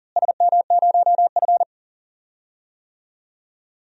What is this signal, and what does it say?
Morse code, heard as a keyed tone, about 39 wpm, 700 Hz. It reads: SM0F
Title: DXCC-100 Call Signs - 4 Characters - Repeated 2x - 40wpm